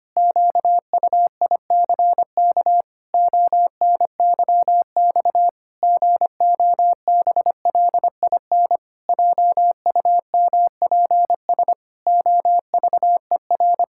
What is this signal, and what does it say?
Morse code, heard as a keyed tone, about 25 wpm, 700 Hz. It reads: QUICK ONYX GOBLIN JUMPS OVER
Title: Pangram Challenge v2 - 25wpm